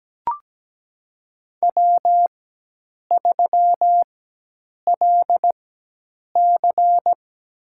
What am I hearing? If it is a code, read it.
Morse code, heard as a keyed tone, about 17 wpm, spaced as for 8 wpm, 700 Hz. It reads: W3LC
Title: USA Call Signs - 4x Character Spacing - 17wpm